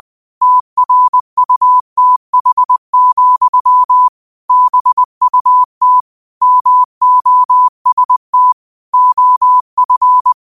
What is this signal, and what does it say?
Morse code, heard as a keyed tone, about 20 wpm, 1000 Hz. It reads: TRUTH, BUT MOST OF